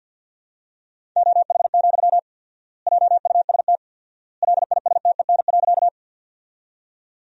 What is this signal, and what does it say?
Morse code, heard as a keyed tone, about 50 wpm, 700 Hz. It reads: OH, JUST LISTEN.
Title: 4x Word-Spacing - Sentences from Top 500 Words - 50wpm